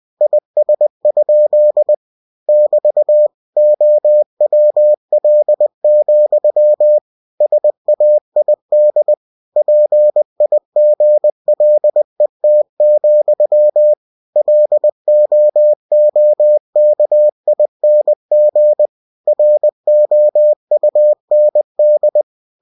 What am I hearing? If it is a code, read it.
Morse code, heard as a keyed tone, about 20 wpm, 600 Hz. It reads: IS? <BT> OWL, SAID PIGLET, LOOKING ROUND
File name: cw-books / winnie-the-pooh / winnie-20wpm-page-0082